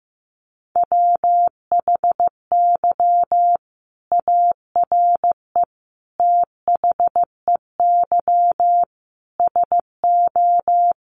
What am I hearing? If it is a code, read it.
Morse code, heard as a keyed tone, about 15 wpm, 700 Hz. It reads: WHY ARE THEY SO
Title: Sentences from Top 1200 Words 15wpm